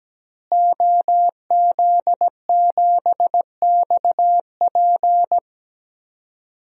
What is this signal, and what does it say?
Morse code, heard as a keyed tone, about 17 wpm, 700 Hz. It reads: OZ7XP
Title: DXCC-100 Call Signs - 5 Characters - Repeated 2x - 17wpm